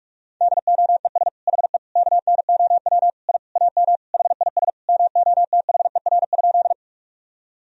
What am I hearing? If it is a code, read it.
Morse code, heard as a keyed tone, about 41 wpm, 700 Hz. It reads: DOES HE KNOW I AM HIS MOTHER?
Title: Sentences from Top 1000 Words - Review Part 2 - 45wpm